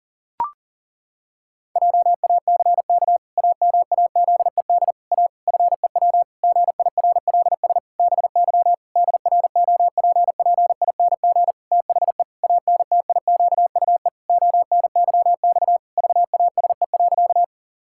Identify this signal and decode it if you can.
Morse code, heard as a keyed tone, about 40 wpm, 700 Hz. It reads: JACK AMAZED A FEW GIRLS BY DROPPING THE ANTIQUE ONYX VASE.
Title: Pangram Challenge v2 - 40wpm